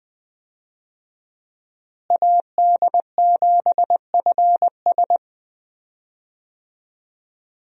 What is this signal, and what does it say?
Morse code, heard as a keyed tone, about 20 wpm, 700 Hz. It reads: AD7FS
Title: USA Call Signs - 5 Characters - Repeated 2x - 20wpm